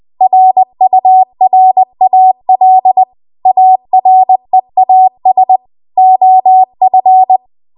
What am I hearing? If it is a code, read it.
Morse code, heard as a keyed tone, about 20 wpm, 750 Hz. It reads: RURAL AREAS OF